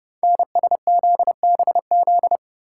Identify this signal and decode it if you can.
Morse code, heard as a keyed tone, about 30 wpm, 690 Hz. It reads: NSZBZ
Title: Code Groups - 5 Letters - 30wpm